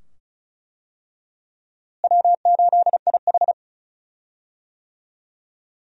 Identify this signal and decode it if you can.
Morse code, heard as a keyed tone, about 34 wpm, 700 Hz. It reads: W8IH
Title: USA Call Signs - 4 Characters - Repeated 2x - 35wpm